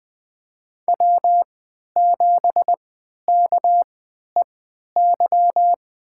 Morse code, heard as a keyed tone, about 20 wpm, spaced as for 11 wpm, 700 Hz. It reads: W7KEY
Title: USA Call Signs - 5 Characters - 3x Character Spacing - 20wpm